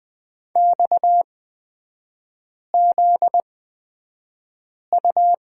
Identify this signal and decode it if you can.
Morse code, heard as a keyed tone, about 20 wpm, spaced as for 5 wpm, 700 Hz. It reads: XZU
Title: 250 Lesson 40e - BK - Three Characters 20wpm